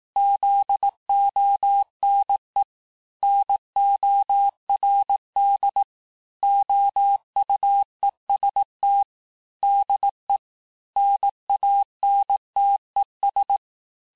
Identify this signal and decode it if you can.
Morse code, heard as a keyed tone, about 18 wpm, 800 Hz. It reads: ZONE NORD OUEST DE NANTES
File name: cours-du-lundi files / cours-1852-FIN-du-QSO-vitesse-17MPM